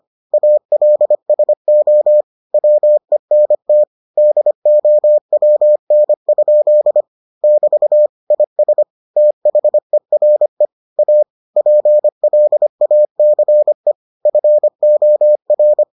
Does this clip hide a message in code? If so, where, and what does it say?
Yes: Morse code from 0.3 to 15.8 s, heard as a keyed tone, about 25 wpm, 600 Hz. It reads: ALSO WENT DOWN? <BT> IS THERE A PLACE FOR